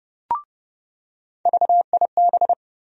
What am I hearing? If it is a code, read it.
Morse code, heard as a keyed tone, about 30 wpm, 700 Hz. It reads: VIB
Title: USA Call Sign Postfixes - 3 Letters 30wpm